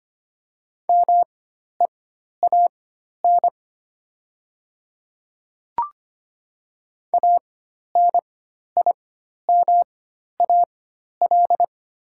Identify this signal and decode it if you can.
Morse code, heard as a keyed tone, about 25 wpm, spaced as for 12 wpm, 700 Hz. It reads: MEAN ANIMAL
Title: 4x Character Spacing - Top 500 Words - 25wpm